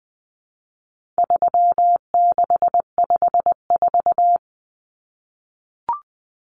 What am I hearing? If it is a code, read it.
Morse code, heard as a keyed tone, about 20 wpm, 700 Hz. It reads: 3654
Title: Common Cut Numbers - 4-Digit 20wpm